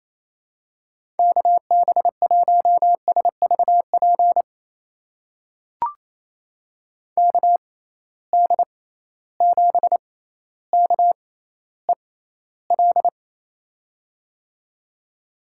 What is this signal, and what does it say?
Morse code, heard as a keyed tone, about 28 wpm, spaced as for 14 wpm, 700 Hz. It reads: KB1SVP KD7KEL
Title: USA Call Signs - 6x Character Spacing - 28wpm